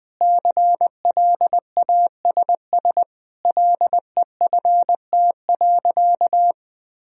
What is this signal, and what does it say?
Morse code, heard as a keyed tone, about 20 wpm, 700 Hz. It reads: CLASS LEFT.
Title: Sentences from Top 1000 Words - Review Part 2 - 20wpm